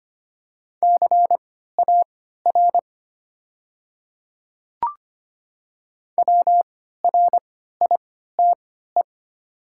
Morse code, heard as a keyed tone, about 25 wpm, spaced as for 14 wpm, 690 Hz. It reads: CAR WRITE
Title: 3x Character Spacing - Top 500 Words - 25wpm